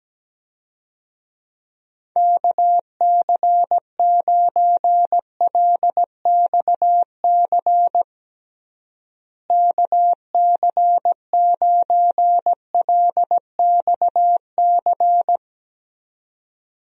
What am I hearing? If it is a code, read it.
Morse code, heard as a keyed tone, about 17 wpm, 700 Hz. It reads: KC9LXC KC9LXC
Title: USA Call Signs - 6 Characters - Repeated 2x - 17wpm